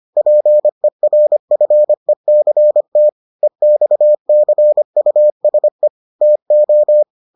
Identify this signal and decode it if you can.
Morse code, heard as a keyed tone, about 25 wpm, 600 Hz. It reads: PERFECT EXCUSE TO